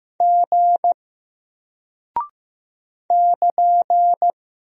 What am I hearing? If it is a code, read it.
Morse code, heard as a keyed tone, about 15 wpm, 700 Hz. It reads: G <KN>
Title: Single Character (Letter:Number:Punctuation:Prosign) Alt 15wpm